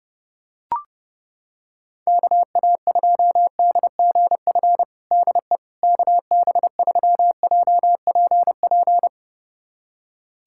Morse code, heard as a keyed tone, about 30 wpm, 700 Hz. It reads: KA2DGF DE KB3JPP
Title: USA Callsign DE Callsign - 6 Characters - 30wpm